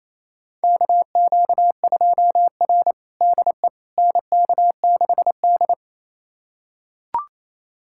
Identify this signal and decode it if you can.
Morse code, heard as a keyed tone, about 28 wpm, 700 Hz. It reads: KQ2R DE NK6D
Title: USA Callsign DE Callsign - 4 Characters - 28wpm